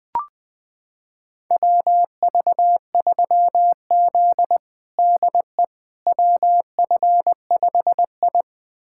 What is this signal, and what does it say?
Morse code, heard as a keyed tone, about 21 wpm, 700 Hz. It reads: WV3Z DE WF5I
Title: USA Callsign DE Callsign - 4 Characters - 20wpm